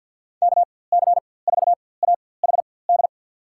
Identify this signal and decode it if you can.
Morse code, heard as a keyed tone, about 53 wpm, spaced as for 24 wpm, 700 Hz. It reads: KC4AHD
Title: USA Call Signs - 6 Characters - 4x Character Spacing - 50wpm